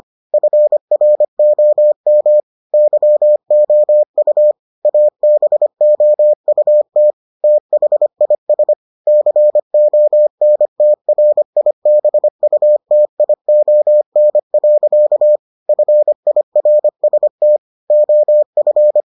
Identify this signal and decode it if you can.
Morse code, heard as a keyed tone, about 25 wpm, 600 Hz. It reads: FROM YOU ABOUT THIS CONTRIBUTION. FIRST OF